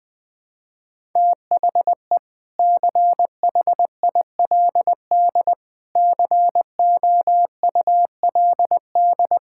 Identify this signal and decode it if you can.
Morse code, heard as a keyed tone, about 20 wpm, 700 Hz. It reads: THE CHILD COULD
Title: Sentences from Top 1000 Words - Review Part 6 - 20wpm